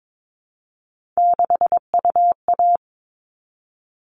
Morse code, heard as a keyed tone, about 22 wpm, 700 Hz. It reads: 6UA
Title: All Cut Numbers - 3-Digit 22wpm